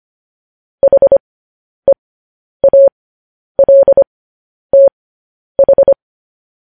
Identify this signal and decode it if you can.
Morse code, heard as a keyed tone, about 25 wpm, spaced as for 10 wpm, 550 Hz. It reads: HEALTH
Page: news-headlines-in-morse-code-at-10-wpm